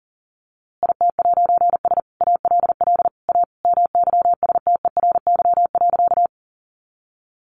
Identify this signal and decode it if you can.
Morse code, heard as a keyed tone, about 40 wpm, 700 Hz. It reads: IT'S ALL A MYSTERY.
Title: A Sound Of Thunder 40wpm